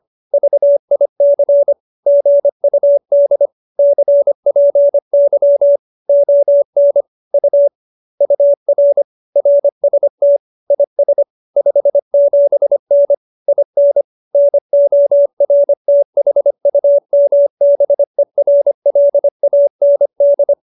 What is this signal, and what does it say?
Morse code, heard as a keyed tone, about 25 wpm, 570 Hz. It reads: VIC GUD CPY ON U UR RST IS 57N IN NORTHUMBERLAND
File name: QSO315 25